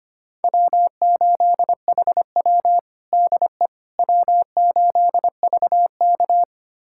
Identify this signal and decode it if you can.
Morse code, heard as a keyed tone, about 25 wpm, 700 Hz. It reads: W8HW DE W8VK
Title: USA Callsign DE Callsign - 4 Characters - 25wpm